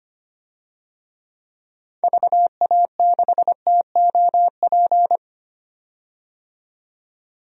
Morse code, heard as a keyed tone, about 25 wpm, 700 Hz. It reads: VA6TOP
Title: DXCC-100 Call Signs - 6 Characters - 25wpm